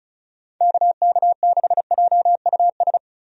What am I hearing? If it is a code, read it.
Morse code, heard as a keyed tone, about 36 wpm, 700 Hz. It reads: KK6JUS
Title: USA Call Signs - 6 Characters - 35wpm